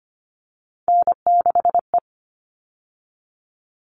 Morse code, heard as a keyed tone, about 25 wpm, 700 Hz. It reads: N6E
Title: All Cut Numbers - 3-Digit 25wpm